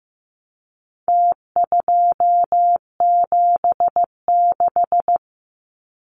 Morse code, heard as a keyed tone, about 15 wpm, 700 Hz. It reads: T276
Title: Numbers - Contest Style - 4-Digit 15wpm